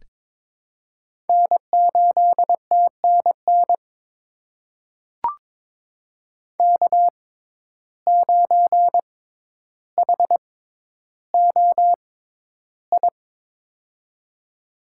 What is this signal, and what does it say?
Morse code, heard as a keyed tone, about 22 wpm, spaced as for 11 wpm, 700 Hz. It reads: N8TNN K9HOI
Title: USA Call Signs - 5 Characters - 6x Character Spacing - 22wpm